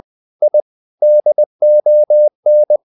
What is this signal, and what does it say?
Morse code, heard as a keyed tone, about 20 wpm, 600 Hz. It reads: I DON
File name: the old man and the sea / the old man and the sea part 1 20wpm